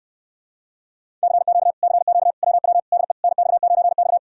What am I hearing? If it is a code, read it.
Morse code, heard as a keyed tone, about 69 wpm, 700 Hz. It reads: CQ CQ WW DE NX9X